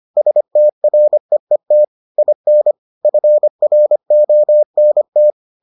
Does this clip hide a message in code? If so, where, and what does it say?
Yes: Morse code from 0.2 to 5.3 s, heard as a keyed tone, about 25 wpm, 600 Hz. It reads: STREET IN FRONT